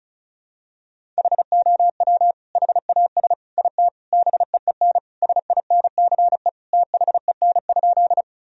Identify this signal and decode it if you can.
Morse code, heard as a keyed tone, about 36 wpm, 700 Hz. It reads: HOW HAS IT BEEN SINCE THEN?
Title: Sentences from Top 1000 Words - Review Part 1 - 35wpm